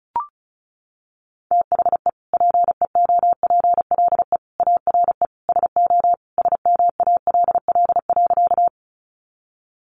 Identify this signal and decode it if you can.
Morse code, heard as a keyed tone, about 34 wpm, 700 Hz. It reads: THE PEOPLE ARE SO SMALL.
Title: Additional Easy Phrases 35wpm